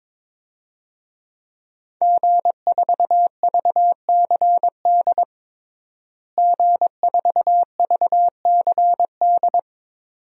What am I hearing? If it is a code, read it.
Morse code, heard as a keyed tone, about 22 wpm, 700 Hz. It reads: G4VCD G4VCD
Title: DXCC-100 Call Signs - 5 Characters - Repeated 2x - 22wpm